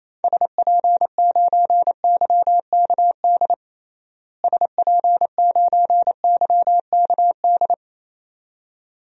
Morse code, heard as a keyed tone, about 28 wpm, 700 Hz. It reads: SP9YKD SP9YKD
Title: DXCC-100 Call Signs - 6 Characters - Repeated 2x - 28wpm